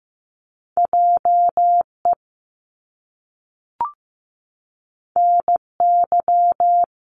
Morse code, heard as a keyed tone, about 15 wpm, 700 Hz. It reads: JE NY
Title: Top Two Letter Combinations in Words 15wpm